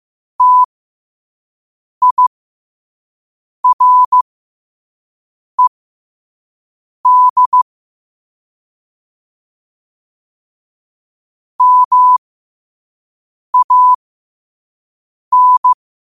Morse code, heard as a keyed tone, about 15 wpm, spaced as for 5 wpm, 1000 Hz. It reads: TIRED MAN
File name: qotdfiles / 5wpm